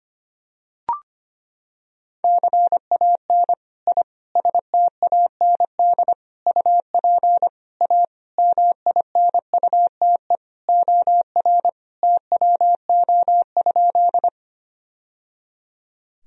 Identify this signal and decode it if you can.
Morse code, heard as a keyed tone, about 25 wpm, 700 Hz. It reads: CAN I STAND UP A MINUTE OR TWO?
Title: Sentences from Top 1000 Words - Review Part 1 - 25wpm